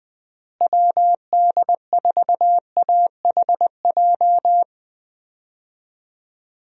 Morse code, heard as a keyed tone, about 20 wpm, 700 Hz. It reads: WD4AHJ
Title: USA Call Signs - 6 Characters - Repeated 2x - 20wpm